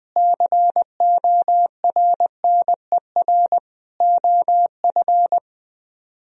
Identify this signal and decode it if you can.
Morse code, heard as a keyed tone, about 20 wpm, 700 Hz. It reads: CORNER OF
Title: Speed-Racing - Sets of 5 Words 20wpm